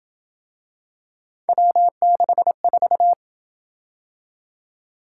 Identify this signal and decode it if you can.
Morse code, heard as a keyed tone, about 27 wpm, 700 Hz. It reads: W64